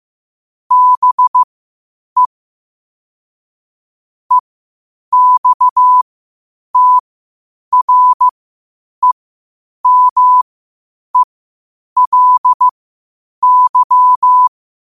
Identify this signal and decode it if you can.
Morse code, heard as a keyed tone, about 15 wpm, spaced as for 8 wpm, 1000 Hz. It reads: BE EXTREMELY
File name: qotdfiles / 8wpm